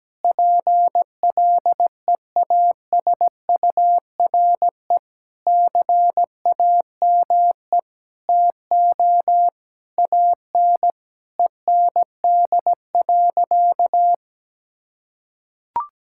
Morse code, heard as a keyed tone, about 17 wpm, 700 Hz. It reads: PLEASURE CAME TO AN END.
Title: Sentences from Top 1600 Words 17wpm